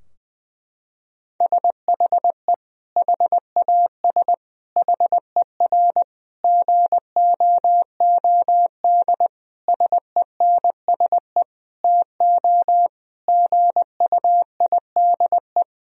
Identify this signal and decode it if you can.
Morse code, heard as a keyed tone, about 19 wpm, 700 Hz. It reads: SHE HAS HER GOOD SENSE TO GUIDE
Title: Sentences from Top 1000 Words - Review Part 4 - 20wpm